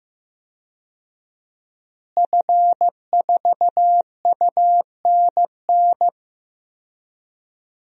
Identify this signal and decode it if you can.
Morse code, heard as a keyed tone, about 15 wpm, 700 Hz. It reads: F4UNN
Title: DXCC-100 Call Signs - 5 Characters - Repeated 2x - 15wpm